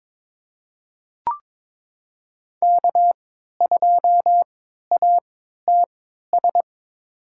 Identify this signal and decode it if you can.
Morse code, heard as a keyed tone, about 22 wpm, spaced as for 12 wpm, 700 Hz. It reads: K2ATS
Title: USA Call Signs - 5 Characters - 3x Character Spacing - 22wpm